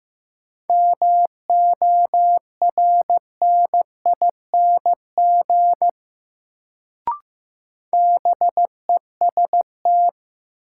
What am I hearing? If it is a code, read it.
Morse code, heard as a keyed tone, about 15 wpm, 700 Hz. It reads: MORNING BEST
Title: Sentences from Top 400 Words - New Word Drill - 15wpm